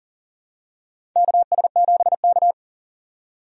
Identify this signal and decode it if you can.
Morse code, heard as a keyed tone, about 40 wpm, 700 Hz. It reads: KS7K
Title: USA Call Signs - 4 Characters - 40wpm